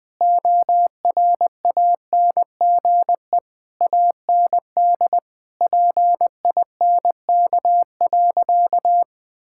Morse code, heard as a keyed tone, about 20 wpm, 700 Hz. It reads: ORANGE AND PINK.
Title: Difficult Sentences with Hints - 20wpm